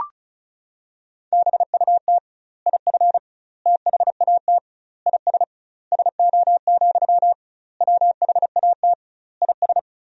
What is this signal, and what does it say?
Morse code, heard as a keyed tone, about 36 wpm, 700 Hz. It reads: BUT IF THAT IS SO, WHAT IS
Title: 2x Word-Spacing - Sentences from Top 200 Words - 35wpm